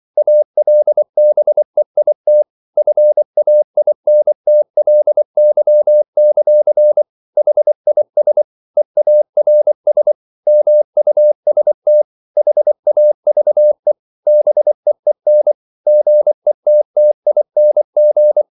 Morse code, heard as a keyed tone, about 24 wpm, 600 Hz. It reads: ALBEIT FAINTLY; HIS EARS MUST HAVE BEEN GETTING